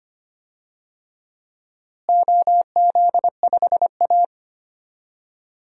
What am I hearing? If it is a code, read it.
Morse code, heard as a keyed tone, about 25 wpm, 700 Hz. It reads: OZ5A
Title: DXCC-100 Call Signs - 4 Characters - Repeated 3x - 25wpm